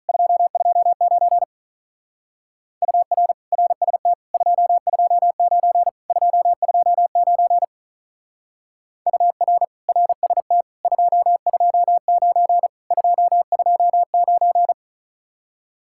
Morse code, heard as a keyed tone, about 44 wpm, 700 Hz. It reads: 229 UR RST 229 229 UR RST 229 229
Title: Speed-Racing Common QSO Phrases - 35wpm